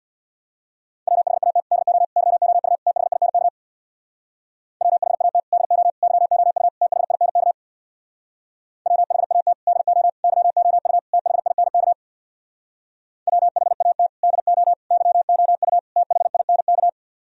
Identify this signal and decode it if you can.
Morse code, heard as a keyed tone, about 64 wpm, 700 Hz. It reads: WHAT DO YOU THINK WHAT DO YOU THINK WHAT DO YOU THINK WHAT DO YOU THINK